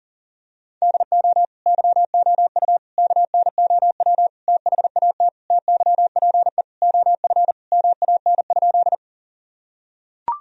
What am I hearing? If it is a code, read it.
Morse code, heard as a keyed tone, about 40 wpm, 700 Hz. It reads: DO YOU KNOW THAT TYPE OF MAN?
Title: Sentences from Top 1000 Words - Review Part 3 - 40wpm